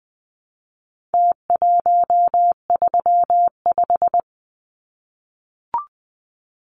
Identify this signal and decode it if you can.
Morse code, heard as a keyed tone, about 20 wpm, 700 Hz. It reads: T135
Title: Numbers - Contest Style - 4-Digit 20wpm